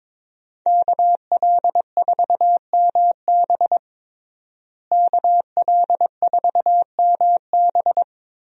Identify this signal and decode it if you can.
Morse code, heard as a keyed tone, about 22 wpm, 700 Hz. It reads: KL4MB KL4MB
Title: USA Call Signs - 5 Characters - Repeated 2x - 22wpm